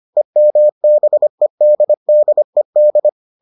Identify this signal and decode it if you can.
Morse code, heard as a keyed tone, about 25 wpm, 600 Hz. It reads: EMBEDDED